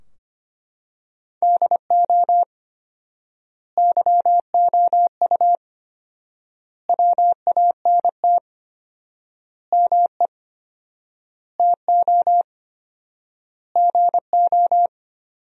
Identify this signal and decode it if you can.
Morse code, heard as a keyed tone, about 25 wpm, 700 Hz. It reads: DO YOU WANT ME TO GO
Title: 4x Word-Spacing - Sentences from Top 200 Words - 25wpm